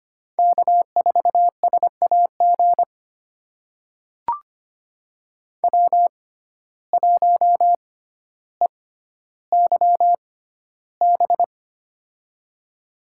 Morse code, heard as a keyed tone, about 25 wpm, spaced as for 13 wpm, 700 Hz. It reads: K4SAG W1EYB
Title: USA Call Signs - 6x Character Spacing - 25wpm